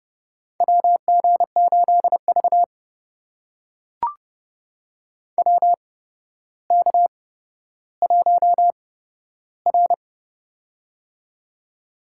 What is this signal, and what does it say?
Morse code, heard as a keyed tone, about 31 wpm, spaced as for 13 wpm, 700 Hz. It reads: WG8V WK1R
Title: USA Call Signs - 4 Characters - 8x Character Spacing - 30wpm